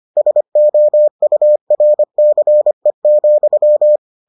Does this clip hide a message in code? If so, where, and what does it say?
Yes: Morse code from 0.2 to 4.0 s, heard as a keyed tone, about 25 wpm, 600 Hz. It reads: SOURCE,